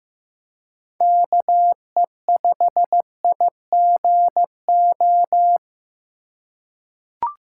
Morse code, heard as a keyed tone, about 15 wpm, 700 Hz. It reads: KE5IGO
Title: USA Call Signs - 6 Characters - 4x Character Spacing - 15wpm